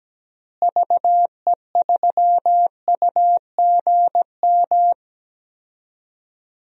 Morse code, heard as a keyed tone, about 17 wpm, 700 Hz. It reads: VE3UGM
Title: DXCC-100 Call Signs - 6 Characters - Repeated 2x - 17wpm